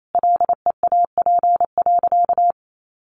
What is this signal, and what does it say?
Morse code, heard as a keyed tone, about 28 wpm, 700 Hz. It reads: LEAP.
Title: A Sound Of Thunder 28wpm